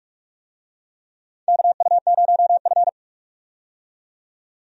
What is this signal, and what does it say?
Morse code, heard as a keyed tone, about 43 wpm, 700 Hz. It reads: KU0F
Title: USA Call Signs - 4 Characters - 45wpm